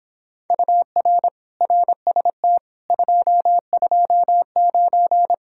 Morse code, heard as a keyed tone, about 26 wpm, 700 Hz. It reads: UR RST 229